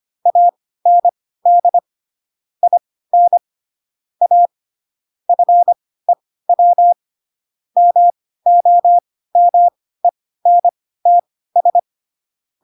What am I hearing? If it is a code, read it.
Morse code, heard as a keyed tone, about 25 wpm, spaced as for 16 wpm, 700 Hz. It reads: AND IN A FEW MOMENTS
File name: Pinocchio / Pinocchio ch5 10 25 ik1hge